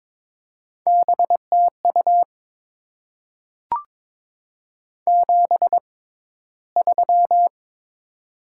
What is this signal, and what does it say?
Morse code, heard as a keyed tone, about 22 wpm, spaced as for 14 wpm, 700 Hz. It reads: BTU 73
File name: Top 20 QSO Elements - 6x Character Spacing - 22wpm